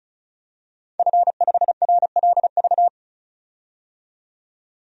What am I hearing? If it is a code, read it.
Morse code, heard as a keyed tone, about 36 wpm, 700 Hz. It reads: F5RLV
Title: DXCC-100 Call Signs - 5 Characters - 35wpm